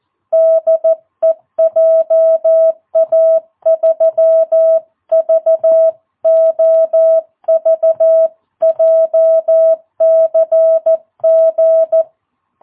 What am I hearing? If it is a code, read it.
Morse code, heard as a keyed tone, about 14 wpm, 660 Hz. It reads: DE JA3VOV JCG
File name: jh8cbh / 2022cw